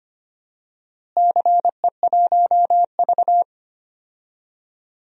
Morse code, heard as a keyed tone, about 25 wpm, 700 Hz. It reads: CE1V
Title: DXCC-100 Call Signs - 4 Characters - Repeated 2x - 25wpm